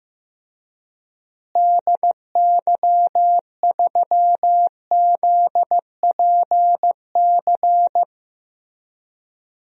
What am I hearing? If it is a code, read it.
Morse code, heard as a keyed tone, about 15 wpm, 700 Hz. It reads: DY3ZPC
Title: DXCC-100 Call Signs - 6 Characters - Repeated 3x - 15wpm